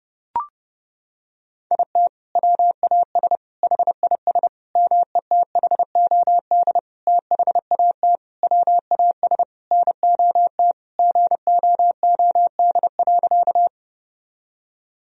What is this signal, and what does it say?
Morse code, heard as a keyed tone, about 30 wpm, 700 Hz. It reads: IT WAS HIS METHOD THAT WAS NOT GOOD.
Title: Sentences from Top 1000 Words - Review Part 3 - 30wpm